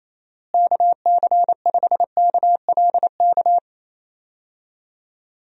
Morse code, heard as a keyed tone, about 28 wpm, 700 Hz. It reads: KC5KLK
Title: USA Call Signs - 6 Characters - Repeated 2x - 28wpm